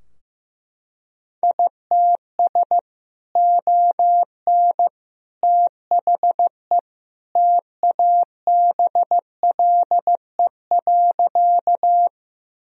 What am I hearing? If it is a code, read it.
Morse code, heard as a keyed tone, about 15 wpm, 700 Hz. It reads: ITS ON THE TABLE.
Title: Sentences from Top 1000 Words - Review Part 2 - 15wpm